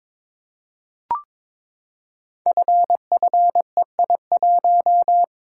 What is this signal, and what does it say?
Morse code, heard as a keyed tone, about 23 wpm, 700 Hz. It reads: FFEI1
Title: Code Groups - 5 Letter:Numbers - 22wpm